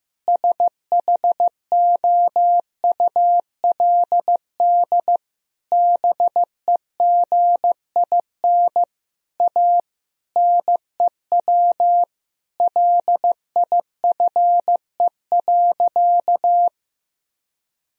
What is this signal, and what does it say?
Morse code, heard as a keyed tone, about 16 wpm, 700 Hz. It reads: SHOULD BEGIN A NEW LIFE.
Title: Sentences from Top 1000 Words - Review Part 1 - 15wpm